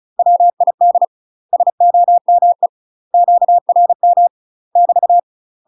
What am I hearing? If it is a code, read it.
Morse code, heard as a keyed tone, about 34 wpm, 700 Hz. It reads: WID SOME QRM <BT>
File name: UK MASTER QSOs TRAINING / 35WPM / UK MASTER QSOs 0001